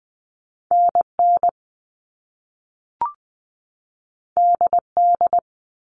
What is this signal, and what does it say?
Morse code, heard as a keyed tone, about 20 wpm, 700 Hz. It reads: NN DD
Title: All Cut Numbers - 2-Digit 20wpm